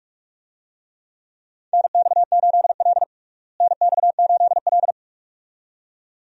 Morse code, heard as a keyed tone, about 46 wpm, 700 Hz. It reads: NX8L NX8L
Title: USA Call Signs - 4 Characters - Repeated 2x - 45wpm